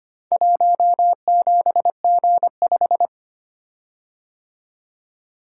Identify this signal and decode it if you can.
Morse code, heard as a keyed tone, about 25 wpm, 700 Hz. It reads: 17G5
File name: practice / 25wpm / Four Letter-Number Combinations - 25wpm